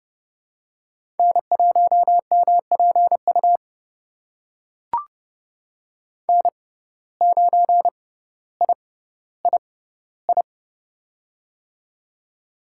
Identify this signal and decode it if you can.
Morse code, heard as a keyed tone, about 29 wpm, spaced as for 15 wpm, 700 Hz. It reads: N1MPU N9III
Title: USA Call Signs - 5 Characters - 6x Character Spacing - 30wpm